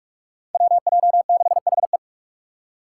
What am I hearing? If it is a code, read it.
Morse code, heard as a keyed tone, about 42 wpm, 700 Hz. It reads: WJ6HE
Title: Five Challenging Letter-Number Combinations 45wpm